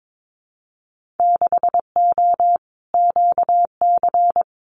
Morse code, heard as a keyed tone, about 22 wpm, 700 Hz. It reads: 6O QC
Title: Winter Field Day Exchanges Encore 22wpm